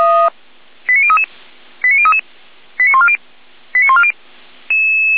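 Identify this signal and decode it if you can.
Five-tone selective call (ZVEI), heard as quick runs of five tones, tones of about 70 ms. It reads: 89033 89033 89140 89140
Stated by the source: ZVEI five-tone selective call received by SDR, feuerwehr-orlamuende.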